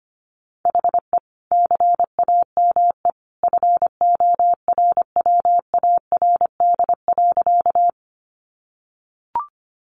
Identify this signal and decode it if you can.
Morse code, heard as a keyed tone, about 25 wpm, 700 Hz. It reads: HE CAME FORWARD.
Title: Sentences from Top 1300 Words 25wpm